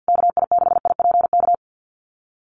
Morse code, heard as a keyed tone, about 50 wpm, 700 Hz. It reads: KI6IPK
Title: USA Call Signs Repeated 2x 50wpm